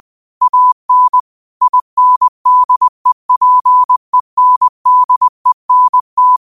Morse code, heard as a keyed tone, about 20 wpm, 1000 Hz. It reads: AN INDEPENDENT